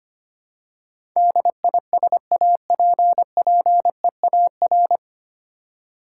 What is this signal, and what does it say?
Morse code, heard as a keyed tone, about 25 wpm, 700 Hz. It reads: DISAPPEAR
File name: Nine Letter Words - 25wpm